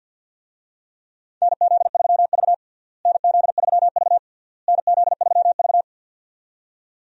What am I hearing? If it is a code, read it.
Morse code, heard as a keyed tone, about 50 wpm, 700 Hz. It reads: NZ3V NZ3V NZ3V
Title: USA Call Signs - 4 Characters - Repeated 3x - 50wpm